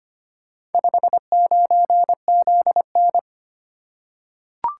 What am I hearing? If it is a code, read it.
Morse code, heard as a keyed tone, about 25 wpm, 700 Hz. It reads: 59ZN
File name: practice / 25wpm / Four Letter-Number Combinations - 25wpm